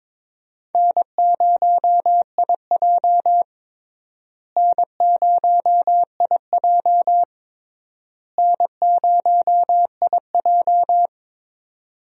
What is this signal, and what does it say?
Morse code, heard as a keyed tone, about 22 wpm, 700 Hz. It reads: N0IJ N0IJ N0IJ
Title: USA Call Signs - 4 Characters - Repeated 3x - 22wpm